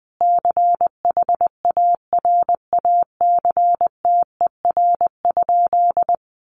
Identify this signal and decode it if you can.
Morse code, heard as a keyed tone, about 20 wpm, 700 Hz. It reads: CHARACTER?
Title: Sentences from Top 1200 Words 20wpm